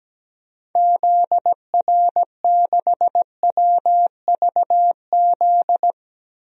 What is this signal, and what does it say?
Morse code, heard as a keyed tone, about 17 wpm, 700 Hz. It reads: ZR6WVZ
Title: DXCC-100 Call Signs - 6 Characters - 17wpm